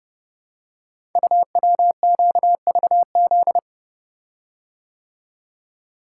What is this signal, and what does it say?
Morse code, heard as a keyed tone, about 30 wpm, 700 Hz. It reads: UWQVZ
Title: Code Groups - 5 Letters - 30wpm